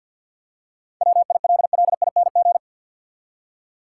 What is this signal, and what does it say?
Morse code, heard as a keyed tone, about 50 wpm, 700 Hz. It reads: WILLING
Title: Seven Letter Words - 50wpm